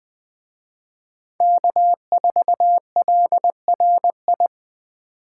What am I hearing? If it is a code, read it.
Morse code, heard as a keyed tone, about 21 wpm, 700 Hz. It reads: K4LRI